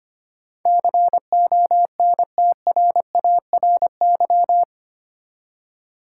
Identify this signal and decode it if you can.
Morse code, heard as a keyed tone, about 25 wpm, 700 Hz. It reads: CONTRARY
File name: 8 Letter Words - First 3 Letters Given - 25wpm